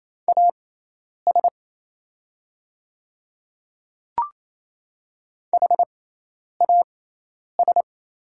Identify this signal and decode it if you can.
Morse code, heard as a keyed tone, about 28 wpm, spaced as for 10 wpm, 700 Hz. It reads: AS HAS
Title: 6x Character Spacing - Top 200 Words - 28wpm